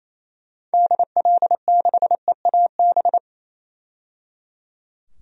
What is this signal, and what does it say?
Morse code, heard as a keyed tone, about 28 wpm, 700 Hz. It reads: DL6EAB
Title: DXCC-100 Call Signs - 6 Characters - 28wpm